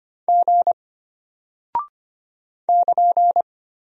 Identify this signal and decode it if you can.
Morse code, heard as a keyed tone, about 25 wpm, 700 Hz. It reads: G <KN>
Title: Single Character (Letter:Number:Punctuation:Prosign) Alt 25wpm